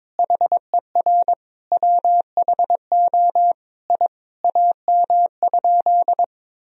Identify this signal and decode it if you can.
Morse code, heard as a keyed tone, about 22 wpm, 700 Hz. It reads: HER WHO I AM?
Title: Sentences from Top 1000 Words - Review Part 2 - 22wpm